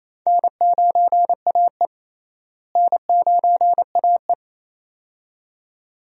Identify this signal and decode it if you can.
Morse code, heard as a keyed tone, about 28 wpm, 700 Hz. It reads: N9AE N9AE
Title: USA Call Signs - 4 Characters - Repeated 2x - 28wpm